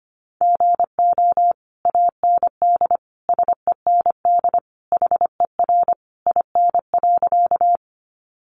Morse code, heard as a keyed tone, about 25 wpm, 700 Hz. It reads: GO AND SEND HER IN.
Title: Sentences from Top 1500 Words 25wpm